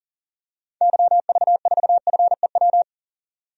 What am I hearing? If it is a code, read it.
Morse code, heard as a keyed tone, about 38 wpm, 700 Hz. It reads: YV4FEW
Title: DXCC-100 Call Signs - 6 Characters - 40wpm